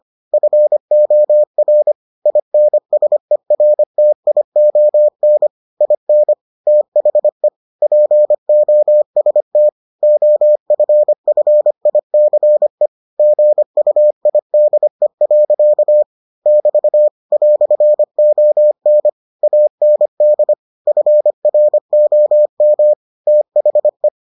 Morse code, heard as a keyed tone, about 25 wpm, 600 Hz. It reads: FOR INSERTION IN THE POST OFFICE GUIDE. <BT> "ON AND FROM THE